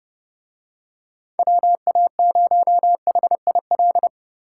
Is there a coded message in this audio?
Yes: Morse code from 1.4 to 4.1 s, heard as a keyed tone, about 30 wpm, 700 Hz. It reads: WA0HIL